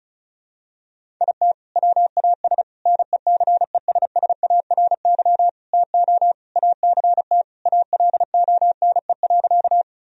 Morse code, heard as a keyed tone, about 36 wpm, 700 Hz. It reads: IT WAS NECESSARY TO ACT ALONE.
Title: Sentences from Top 1000 Words - Review Part 1 - 35wpm